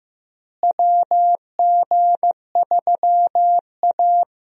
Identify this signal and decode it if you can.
Morse code, heard as a keyed tone, about 15 wpm, 700 Hz. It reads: WG3A
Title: USA Call Signs - 4 Characters - Repeated 3x - 15wpm